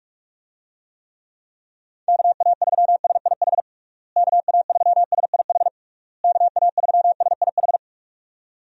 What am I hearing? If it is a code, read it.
Morse code, heard as a keyed tone, about 46 wpm, 700 Hz. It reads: KA3SIH KA3SIH KA3SIH
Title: USA Call Signs - 6 Characters - Repeated 3x - 45wpm